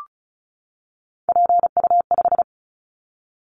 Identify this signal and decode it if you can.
Morse code, heard as a keyed tone, about 36 wpm, 690 Hz. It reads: PU5
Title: Callsign Prefixes for Top 50 DXCC - 35wpm